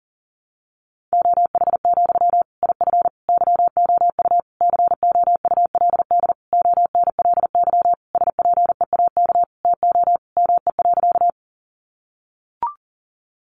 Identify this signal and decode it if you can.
Morse code, heard as a keyed tone, about 40 wpm, 700 Hz. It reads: OH, IF YOU COULD ONLY SPEAK TO ME.
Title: Sentences from Top 1100 Words 40wpm